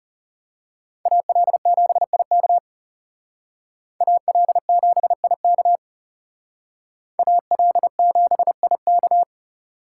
Morse code, heard as a keyed tone, about 35 wpm, 700 Hz. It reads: AL7IK AL7IK AL7IK
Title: Speed Racing - USA 2x2 Call Signs 30wpm